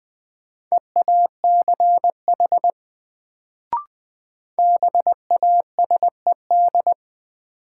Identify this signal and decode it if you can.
Morse code, heard as a keyed tone, about 20 wpm, 700 Hz. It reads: EACH BASED
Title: Sets of 2 Words - New Word Drill - 20wpm